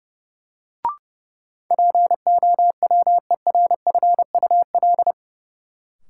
Morse code, heard as a keyed tone, about 30 wpm, 700 Hz. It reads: POWERFUL
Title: Sentences from Top 1300 Words - New Word Drill Alt - 30wpm